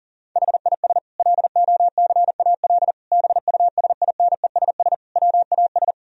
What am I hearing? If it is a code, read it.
Morse code, heard as a keyed tone, about 40 wpm, 700 Hz. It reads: HIS LOCAL BUSINESS WAS
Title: Sentences from Top 1000 Words - Review Part 3 - 40wpm